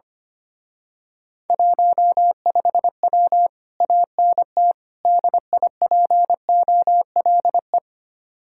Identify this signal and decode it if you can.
Morse code, heard as a keyed tone, about 25 wpm, 700 Hz. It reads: 15W ANT DIPOLE
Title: Speed-Racing Common QSO Phrases Encore - 25wpm